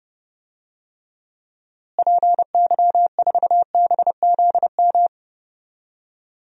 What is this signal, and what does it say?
Morse code, heard as a keyed tone, about 30 wpm, 700 Hz. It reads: PY4BZM
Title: DXCC-100 Call Signs - 6 Characters - Repeated 3x - 30wpm